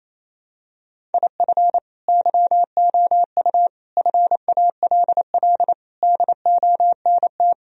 Morse code, heard as a keyed tone, about 28 wpm, 700 Hz. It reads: IF YOU FALL DONT
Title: Sentences from Top 1000 Words - Review Part 2 - 28wpm